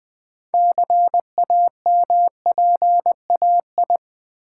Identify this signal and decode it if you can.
Morse code, heard as a keyed tone, about 20 wpm, 700 Hz. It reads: CAMPAI
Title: Build A Word - 3 Letters At A Time - 8-Letter Words - 20wpm